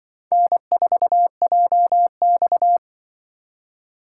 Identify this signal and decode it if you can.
Morse code, heard as a keyed tone, about 24 wpm, 700 Hz. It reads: N4JX
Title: Speed Racing - USA 1x2 Call Signs 20wpm